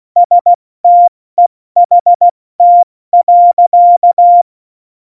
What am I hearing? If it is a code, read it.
Morse code, heard as a keyed tone, about 16 wpm, 700 Hz. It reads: STEHT.
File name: Morsekurs / Witze / Witz 13. November 2018, 70 F80